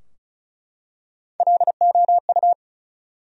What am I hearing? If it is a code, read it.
Morse code, heard as a keyed tone, about 36 wpm, 700 Hz. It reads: LOU
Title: USA Common Names - Max 5 Letters - 35wpm